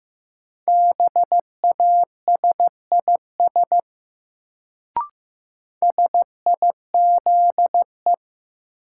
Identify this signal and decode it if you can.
Morse code, heard as a keyed tone, about 15 wpm, 700 Hz. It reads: BASIS SIZE
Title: Sets of 3 Words - New Word Drill - 15wpm